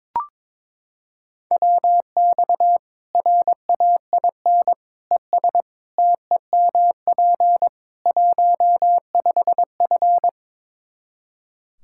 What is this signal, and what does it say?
Morse code, heard as a keyed tone, about 21 wpm, 700 Hz. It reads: WX RAIN ES TEMP 15F
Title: Practice QSO - Exchange 2 - Info - 22wpm